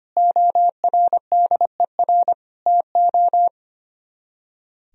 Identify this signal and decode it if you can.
Morse code, heard as a keyed tone, about 30 wpm, 700 Hz. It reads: ORDER TO
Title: Speed-Racing - Sets of 3 Words 25wpm